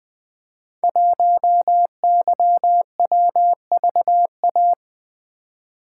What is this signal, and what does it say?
Morse code, heard as a keyed tone, about 20 wpm, 700 Hz. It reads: 1YWVA
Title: Code Groups - 5 Letter:Numbers - 20wpm